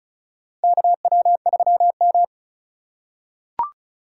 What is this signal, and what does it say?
Morse code, heard as a keyed tone, about 36 wpm, 700 Hz. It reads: KW3M
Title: USA Call Signs - 4 Characters - 2x Character Spacing - 35wpm